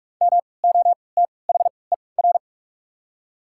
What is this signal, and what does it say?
Morse code, heard as a keyed tone, about 48 wpm, spaced as for 26 wpm, 700 Hz. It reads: MOTHER
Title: 3x Character Spacing - Top 200 Words - 45wpm